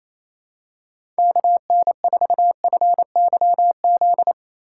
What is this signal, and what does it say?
Morse code, heard as a keyed tone, about 28 wpm, 700 Hz. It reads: KN4FYZ
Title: USA Call Signs - 6 Characters - 4x Character Spacing - 28wpm